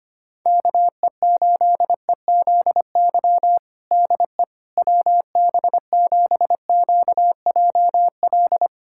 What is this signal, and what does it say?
Morse code, heard as a keyed tone, about 25 wpm, 700 Hz. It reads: KE8EZY DE WB7QJL
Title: USA Callsign DE Callsign - 6 Characters - 25wpm